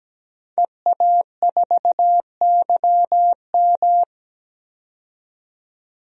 Morse code, heard as a keyed tone, about 17 wpm, 700 Hz. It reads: EA4YM
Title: DXCC-100 Call Signs - 5 Characters - 17wpm